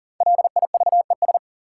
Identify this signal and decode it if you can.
Morse code, heard as a keyed tone, about 40 wpm, 700 Hz. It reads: LIVES
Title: Sets of 5 Words Encore - New Word Drill - 40wpm